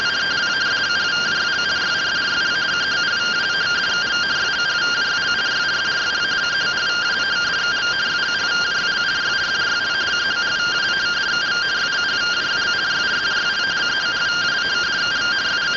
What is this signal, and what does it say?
Two-tone teleprinter signal, about 170 Hz shift.